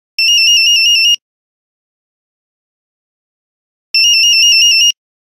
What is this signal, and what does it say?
There is no code.